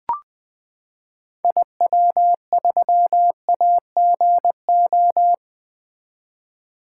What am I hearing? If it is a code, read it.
Morse code, heard as a keyed tone, about 20 wpm, 700 Hz. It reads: IW3AGO
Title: 2023 SOTA Activator Call Signs - 20wpm